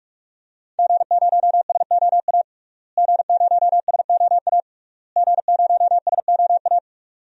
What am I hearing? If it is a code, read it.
Morse code, heard as a keyed tone, about 42 wpm, 700 Hz. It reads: G0SOA G0SOA G0SOA
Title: DXCC-100 Call Signs - 5 Characters - Repeated 3x - 45wpm